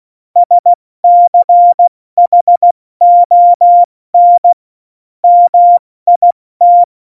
Morse code, heard as a keyed tone, about 16 wpm, 700 Hz. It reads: SCHON MIT
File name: Morsekurs / Witze / Witz 25. September 2018, 70 F80